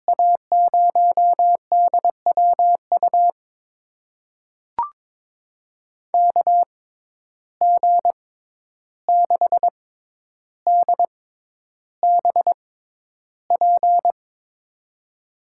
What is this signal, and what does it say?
Morse code, heard as a keyed tone, about 22 wpm, spaced as for 11 wpm, 700 Hz. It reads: A0DWU KG6DBP
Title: USA Call Signs - 6 Characters - 6x Character Spacing - 22wpm